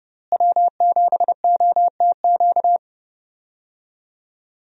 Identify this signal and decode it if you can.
Morse code, heard as a keyed tone, about 32 wpm, 700 Hz. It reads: W7OTQ
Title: USA Call Signs - 5 Characters - Repeated 2x - 30wpm